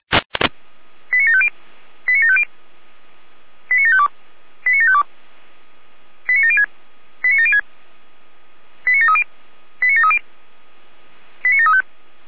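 Five-tone selective call (ZVEI), heard as quick runs of five tones, tones of about 80 ms. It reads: 89750 89750 89752 89752 89706 89706 89730 89730 89735
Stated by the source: ZVEI five-tone selective call received on an SDR.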